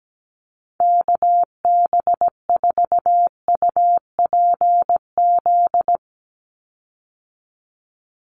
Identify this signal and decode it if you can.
Morse code, heard as a keyed tone, about 17 wpm, 700 Hz. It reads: KB4UPZ
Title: USA Call Signs 17wpm